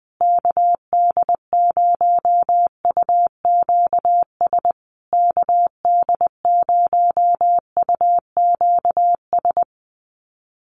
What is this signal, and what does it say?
Morse code, heard as a keyed tone, about 20 wpm, 700 Hz. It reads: KD0UQS KD0UQS
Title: USA Call Signs Repeated 3x 20wpm